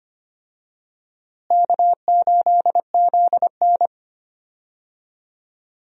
Morse code, heard as a keyed tone, about 25 wpm, 700 Hz. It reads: K8ZN
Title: USA Call Signs - 4 Characters - Repeated 3x - 25wpm